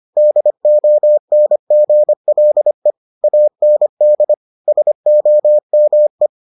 Morse code, heard as a keyed tone, about 25 wpm, 600 Hz. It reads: DONGLE AND SOME